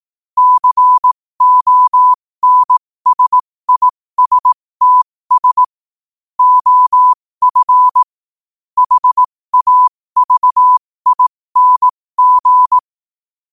Morse code, heard as a keyed tone, about 18 wpm, spaced as for 15 wpm, 1000 Hz. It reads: CONSISTS OF HAVING
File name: qotdfiles / 15wpm